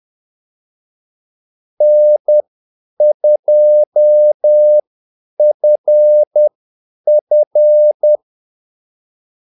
Morse code, heard as a keyed tone, about 10 wpm, spaced as for 8 wpm, 600 Hz. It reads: N2FF
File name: Callsigns-10-8wpm